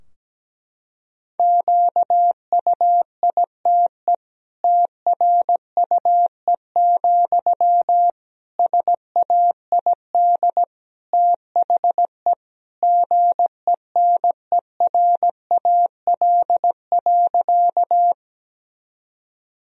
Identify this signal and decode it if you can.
Morse code, heard as a keyed tone, about 17 wpm, 700 Hz. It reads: QUITE TRUE, SAID THE GENERAL.
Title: Sentences from Top 1000 Words - Review Part 3 - 17wpm